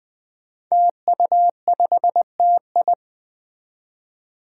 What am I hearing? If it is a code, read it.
Morse code, heard as a keyed tone, about 20 wpm, 700 Hz. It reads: TU5TI
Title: Five Challenging Letter-Number Combinations 20wpm